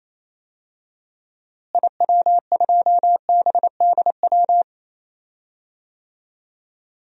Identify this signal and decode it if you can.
Morse code, heard as a keyed tone, about 28 wpm, 700 Hz. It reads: IW2BDW
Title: DXCC-100 Call Signs - 6 Characters - Repeated 3x - 28wpm